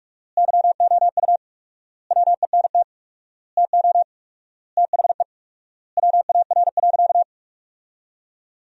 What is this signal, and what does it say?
Morse code, heard as a keyed tone, about 46 wpm, 700 Hz. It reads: YOU WENT TO THE WAR.
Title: 4x Word-Spacing - Sentences from Top 400 Words - 45wpm